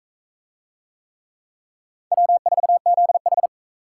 Morse code, heard as a keyed tone, about 38 wpm, 700 Hz. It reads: W4ZH